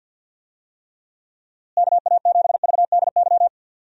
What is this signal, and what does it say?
Morse code, heard as a keyed tone, about 50 wpm, 700 Hz. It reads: KA7VDY
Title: USA Call Signs - 6 Characters - Repeated 2x - 50wpm